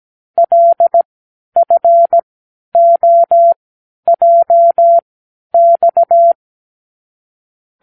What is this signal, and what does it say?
Morse code, heard as a keyed tone, about 17 wpm, spaced as for 10 wpm, 680 Hz. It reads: LFOJX
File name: k7qo / Lesson 36 (A-X)